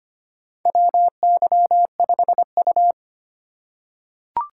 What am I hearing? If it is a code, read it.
Morse code, heard as a keyed tone, about 25 wpm, 700 Hz. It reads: WY5U
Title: USA Call Signs - 4 Characters - 4x Character Spacing - 25wpm